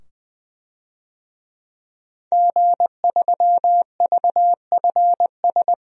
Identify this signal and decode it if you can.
Morse code, heard as a keyed tone, about 20 wpm, 700 Hz. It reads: G3VFS
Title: DXCC-100 Call Signs - 5 Characters - 20wpm